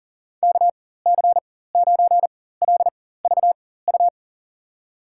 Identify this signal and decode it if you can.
Morse code, heard as a keyed tone, about 39 wpm, spaced as for 18 wpm, 700 Hz. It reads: KC9LVU
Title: USA Call Signs - 6 Characters - 4x Character Spacing - 40wpm